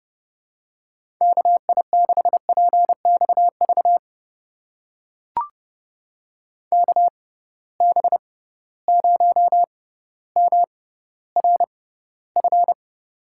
Morse code, heard as a keyed tone, about 32 wpm, spaced as for 16 wpm, 700 Hz. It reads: KI6PXV KB0MRF
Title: USA Call Signs - 6 Characters - 6x Character Spacing - 30wpm